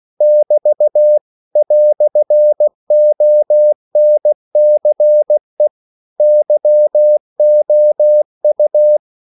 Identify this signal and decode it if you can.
Morse code, heard as a keyed tone, about 16 wpm, 600 Hz. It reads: <BT> "ONCE YOU
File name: CW / Practice / EnglishQuotations0216